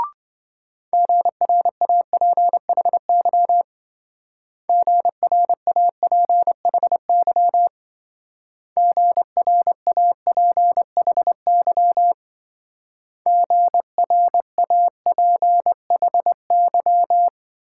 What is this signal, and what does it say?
Morse code, heard as a keyed tone, about 26 wpm, 700 Hz. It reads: GRAPHY GRAPHY GRAPHY GRAPHY